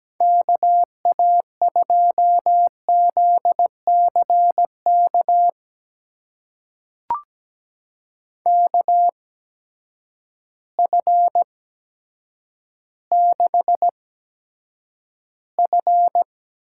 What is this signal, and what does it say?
Morse code, heard as a keyed tone, about 17 wpm, spaced as for 9 wpm, 700 Hz. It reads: KA2ZCK KF6F
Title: USA Call Signs - 6 Characters - 8x Character Spacing - 17wpm